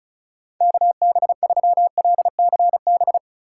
Morse code, heard as a keyed tone, about 34 wpm, 700 Hz. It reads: KB3LCB
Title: USA Call Signs - 6 Characters - 3x Character Spacing - 35wpm